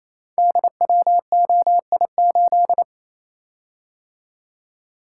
Morse code, heard as a keyed tone, about 28 wpm, 700 Hz. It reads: DWOI8
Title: Code Groups - 5 Letter:Numbers - 28wpm